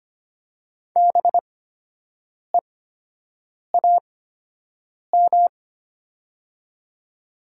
Morse code, heard as a keyed tone, about 25 wpm, spaced as for 7 wpm, 700 Hz. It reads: BEAM
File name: Top 90 QSO Elements - 8x Character Spacing - 25wpm